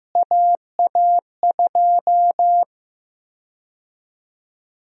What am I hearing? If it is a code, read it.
Morse code, heard as a keyed tone, about 15 wpm, 700 Hz. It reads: AA2
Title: USA Call Sign Prefixes - Repeated 2x - 15wpm